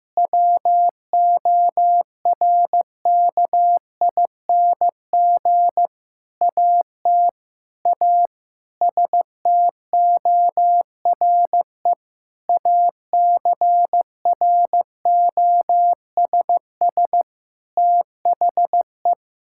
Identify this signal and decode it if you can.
Morse code, heard as a keyed tone, about 15 wpm, 700 Hz. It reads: WORKING AT A STORE ACROSS THE
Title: Sentences from Top 1000 Words - Review Part 5 - 15wpm